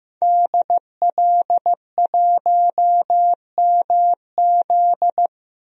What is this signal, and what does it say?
Morse code, heard as a keyed tone, about 15 wpm, 700 Hz. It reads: DL1MZ
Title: DXCC-100 Call Signs - 5 Characters - 15wpm